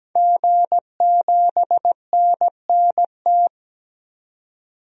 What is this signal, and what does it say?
Morse code, heard as a keyed tone, about 17 wpm, 700 Hz. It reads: G7NNT
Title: DXCC-100 Call Signs - 5 Characters - 17wpm